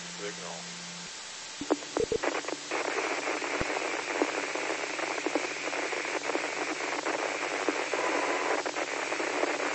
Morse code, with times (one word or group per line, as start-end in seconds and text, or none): none